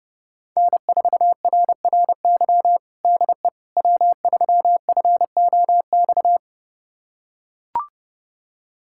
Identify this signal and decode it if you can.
Morse code, heard as a keyed tone, about 30 wpm, 700 Hz. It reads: N4RRY DE W3FOX
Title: USA Callsign DE Callsign - 5 Characters - 30wpm